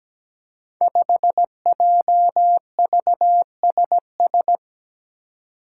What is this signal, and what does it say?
Morse code, heard as a keyed tone, about 17 wpm, 700 Hz. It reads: 5JVSS
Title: Five Challenging Letter-Number Combinations 17wpm